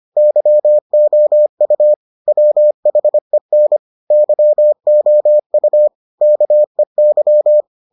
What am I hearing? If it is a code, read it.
Morse code, heard as a keyed tone, about 25 wpm, 600 Hz. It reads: YOU WHEN YOU KEY